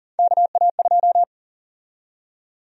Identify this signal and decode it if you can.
Morse code, heard as a keyed tone, about 40 wpm, 700 Hz. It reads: KA2
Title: USA Call Sign Prefixes - Repeated 2x - 40wpm